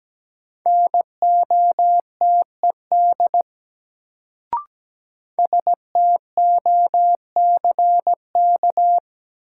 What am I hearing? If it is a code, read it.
Morse code, heard as a keyed tone, about 17 wpm, 700 Hz. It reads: NOTED STOCK
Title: Sets of 5 Words - New Word Drill - 17wpm